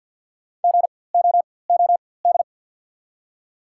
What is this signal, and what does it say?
Morse code, heard as a keyed tone, about 53 wpm, spaced as for 24 wpm, 690 Hz. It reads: GOOD
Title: 4x Character Spacing - Top 300 Words - 50wpm